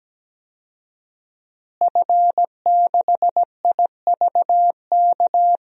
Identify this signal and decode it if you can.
Morse code, heard as a keyed tone, about 17 wpm, 700 Hz. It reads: F6IVK
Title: DXCC-100 Call Signs - 5 Characters - Repeated 3x - 17wpm